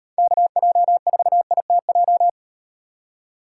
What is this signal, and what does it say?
Morse code, heard as a keyed tone, about 38 wpm, 700 Hz. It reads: KJ4ITJ